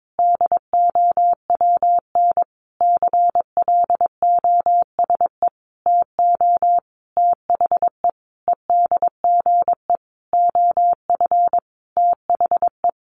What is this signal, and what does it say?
Morse code, heard as a keyed tone, about 22 wpm, 700 Hz. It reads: DOWN CLOSE TO THE EDGE OF THE
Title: Sentences from Top 1300 Words 22wpm